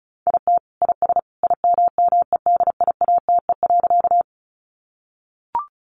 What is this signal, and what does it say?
Morse code, heard as a keyed tone, about 34 wpm, 700 Hz. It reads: IT IS IMMEDIATE.
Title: Sentences from Top 1200 Words 35wpm